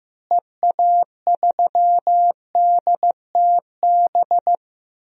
Morse code, heard as a keyed tone, about 15 wpm, 700 Hz. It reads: EA3DTB
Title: DXCC-100 Call Signs - 6 Characters - Repeated 2x - 15wpm